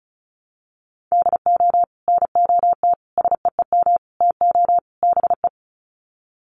Morse code, heard as a keyed tone, about 34 wpm, 700 Hz. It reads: DO NOT SEEM TO BE
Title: Sets of 5 Words Encore2 35wpm